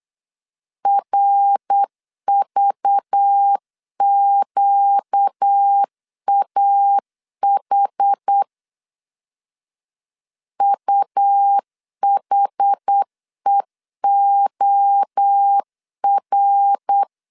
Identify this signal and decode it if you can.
Morse code, heard as a keyed tone, about 8 wpm, 800 Hz. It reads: RVQAH UHEOR
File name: LEZIONI / 6 PKX0 / 6 PKX0 8wpm 3341